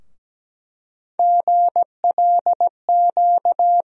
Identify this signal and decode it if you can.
Morse code, heard as a keyed tone, about 17 wpm, 700 Hz. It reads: GLQ
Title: USA Call Sign Postfixes - 3 Letters 17wpm